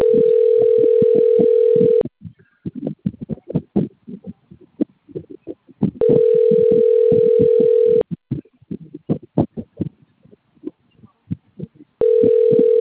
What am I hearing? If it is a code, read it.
Ringback tone (North America), 440 + 480 Hz.